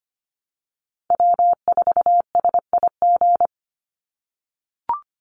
Morse code, heard as a keyed tone, about 25 wpm, 700 Hz. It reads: W4SIG
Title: Call Signs - 25wpm